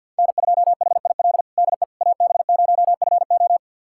Morse code, heard as a keyed tone, about 48 wpm, 700 Hz. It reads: N2HIL DE AB0FO
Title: USA Callsign DE Callsign - 5 Characters - 50wpm